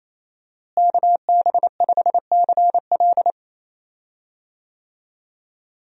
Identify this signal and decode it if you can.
Morse code, heard as a keyed tone, about 27 wpm, 700 Hz. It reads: KB5CL
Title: USA Call Signs - 5 Characters - Repeated 3x - 28wpm